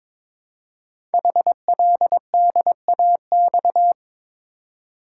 Morse code, heard as a keyed tone, about 22 wpm, 700 Hz. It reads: HLDAX
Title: Code Groups - 5 Letters - 22wpm